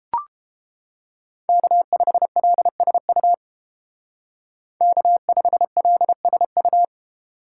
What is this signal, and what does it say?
Morse code, heard as a keyed tone, about 30 wpm, 700 Hz. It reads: K5LSU K5LSU